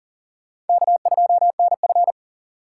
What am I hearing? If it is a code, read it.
Morse code, heard as a keyed tone, about 54 wpm, spaced as for 47 wpm, 700 Hz. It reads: K2NF
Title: Speed Racing - USA Call Signs 40wpm